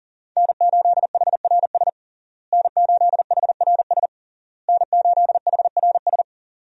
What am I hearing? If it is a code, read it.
Morse code, heard as a keyed tone, about 37 wpm, 700 Hz. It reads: N8HRS N8HRS N8HRS
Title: USA Call Signs - 5 Characters - Repeated 3x - 40wpm